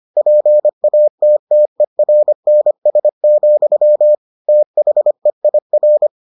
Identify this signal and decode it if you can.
Morse code, heard as a keyed tone, about 25 wpm, 600 Hz. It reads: PATTERNS, THEIR